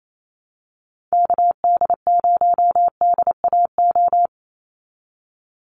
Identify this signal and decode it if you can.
Morse code, heard as a keyed tone, about 28 wpm, 700 Hz. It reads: KD0DAO
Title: USA Call Signs 28wpm